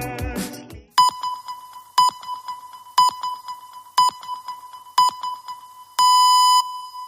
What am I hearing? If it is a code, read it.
Radio time signal (time pips). It is heard from 1.0 to 6.6 s.